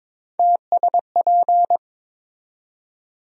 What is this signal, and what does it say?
Morse code, heard as a keyed tone, about 22 wpm, 700 Hz. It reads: TSP
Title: USA Call Sign Postfixes - 3 Letters 22wpm